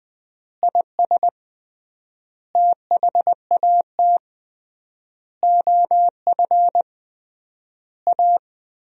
Morse code, heard as a keyed tone, about 20 wpm, 700 Hz. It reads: IS THAT OF A
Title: 3x Word-Spacing - Sentences from Top 500 Words - 20wpm